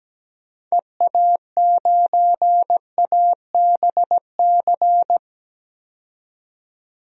Morse code, heard as a keyed tone, about 17 wpm, 700 Hz. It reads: EA9ABC
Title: DXCC-100 Call Signs - 6 Characters - 17wpm